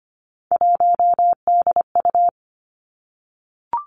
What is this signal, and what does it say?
Morse code, heard as a keyed tone, about 25 wpm, 700 Hz. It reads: 1DU
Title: Three Challenging Letter-Number Combinations 25wpm